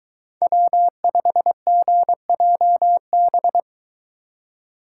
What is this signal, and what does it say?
Morse code, heard as a keyed tone, about 22 wpm, 700 Hz. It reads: W5GJB